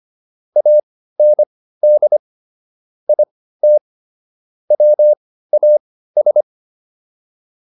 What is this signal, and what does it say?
Morse code, heard as a keyed tone, about 25 wpm, spaced as for 15 wpm, 600 Hz. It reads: AND IT WAS
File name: WD205 15